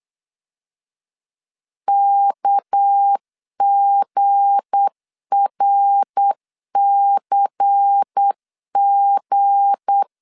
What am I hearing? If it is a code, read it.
Morse code, heard as a keyed tone, about 8 wpm, 800 Hz. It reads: KGRCG